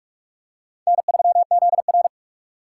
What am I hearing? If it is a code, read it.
Morse code, heard as a keyed tone, about 42 wpm, 700 Hz. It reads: N3ZR